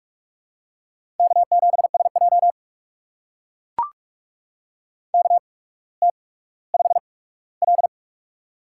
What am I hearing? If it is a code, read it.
Morse code, heard as a keyed tone, about 44 wpm, spaced as for 19 wpm, 700 Hz. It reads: K7SJ KT5L
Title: USA Call Signs - 4 Characters - 8x Character Spacing - 45wpm